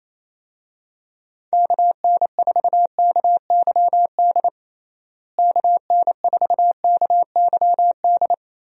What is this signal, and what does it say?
Morse code, heard as a keyed tone, about 28 wpm, 700 Hz. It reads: KN4KYD KN4KYD
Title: USA Call Signs - 6 Characters - Repeated 2x - 28wpm